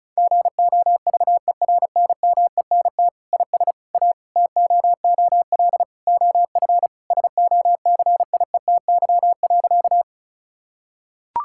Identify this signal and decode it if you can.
Morse code, heard as a keyed tone, about 36 wpm, 700 Hz. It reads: GOVERNMENT IS A TOOL OF SOCIETY.
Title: Sentences from Top 1000 Words - Review Part 6 - 35wpm